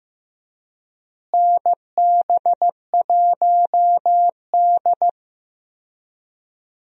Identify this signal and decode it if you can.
Morse code, heard as a keyed tone, about 15 wpm, 700 Hz. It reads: NB1D
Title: USA Call Signs - 4 Characters - Repeated 3x - 15wpm